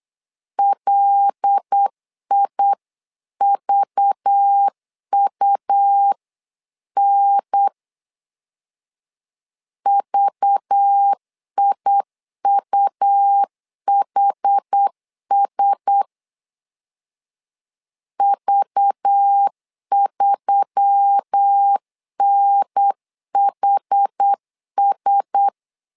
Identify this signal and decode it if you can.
Morse code, heard as a keyed tone, about 8 wpm, 800 Hz. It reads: LIVUN VIUHS V3NHS
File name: LEZIONI / 5 LBVC / 5 LBV 8wpm 2837